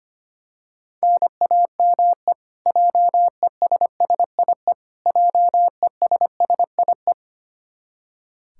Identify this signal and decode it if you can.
Morse code, heard as a keyed tone, about 25 wpm, 700 Hz. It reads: NAME JESSIE JESSIE
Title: Practice QSO - Exchange 1 - Beginning - 25wpm